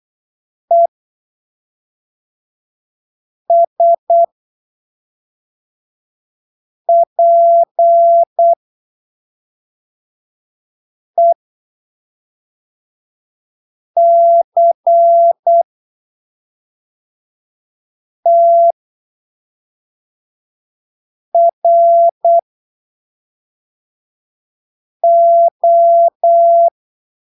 Morse code, heard as a keyed tone, about 8 wpm, spaced as for 3 wpm, 680 Hz. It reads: ESPECTRO
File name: arquivos / texto / texto 06 08ppm